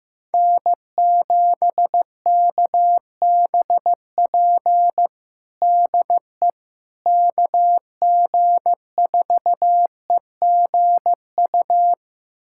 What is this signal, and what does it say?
Morse code, heard as a keyed tone, about 16 wpm, 700 Hz. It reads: N7KBP DE KG4EGU
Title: USA Callsign DE Callsign 15wpm